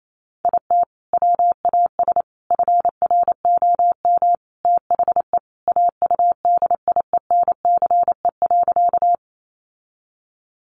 Morse code, heard as a keyed tone, about 28 wpm, 700 Hz. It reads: IT WAS FROM THE AUDIENCE.
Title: Sentences from Top 800 Words 28wpm